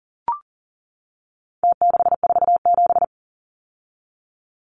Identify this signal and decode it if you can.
Morse code, heard as a keyed tone, about 40 wpm, 700 Hz. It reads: T647
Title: Numbers - Contest Style - 4-Digit 40wpm